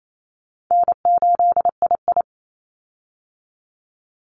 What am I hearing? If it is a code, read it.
Morse code, heard as a keyed tone, about 28 wpm, 700 Hz. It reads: N8II
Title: Easy Call Signs Repeated 2x - 28wpm